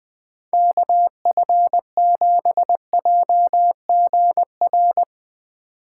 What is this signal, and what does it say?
Morse code, heard as a keyed tone, about 20 wpm, 700 Hz. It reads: KF7JGR
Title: Speed Racing - USA 2x3 Call Signs 17wpm